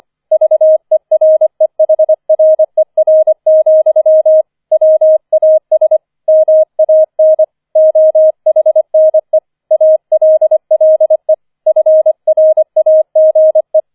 Morse code, heard as a keyed tone, about 25 wpm, 630 Hz. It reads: VEREHRER, WAS MAN OHNE ALLE FRAGE